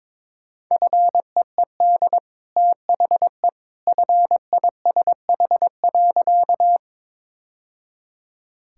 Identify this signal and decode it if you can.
Morse code, heard as a keyed tone, about 22 wpm, 690 Hz. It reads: FEED THE FISH.
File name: practice / 22wpm / Easy Sentences with Hints - 22wpm